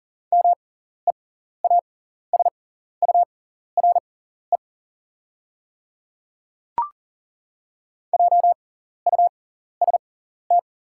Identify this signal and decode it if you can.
Morse code, heard as a keyed tone, about 40 wpm, spaced as for 14 wpm, 700 Hz. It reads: MEASURE JUST
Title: 6x Character Spacing - Top 500 Words - 40wpm